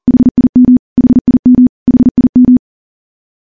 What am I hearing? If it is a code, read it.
Morse code, heard as a keyed tone, about 43 wpm, 250 Hz. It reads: HIM HIM HIM
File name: morse / morse-training-40WPM